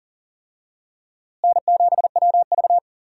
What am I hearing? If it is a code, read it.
Morse code, heard as a keyed tone, about 40 wpm, 700 Hz. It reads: N7WV
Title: USA Call Signs - 4 Characters - Repeated 2x - 40wpm